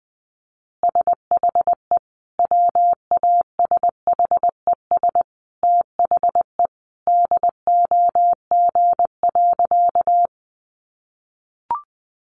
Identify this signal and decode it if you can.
Morse code, heard as a keyed tone, about 20 wpm, 700 Hz. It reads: SHE WASHES THE DOG.
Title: Easy Phrases 20wpm